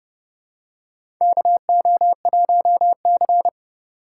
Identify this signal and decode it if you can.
Morse code, heard as a keyed tone, about 31 wpm, 700 Hz. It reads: KO1C
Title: USA Call Signs - 4 Characters - 3x Character Spacing - 30wpm